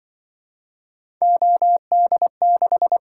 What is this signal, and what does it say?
Morse code, heard as a keyed tone, about 24 wpm, 700 Hz. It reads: OD6
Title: Speed-Racing - Three Letter-Number Combinations - 20wpm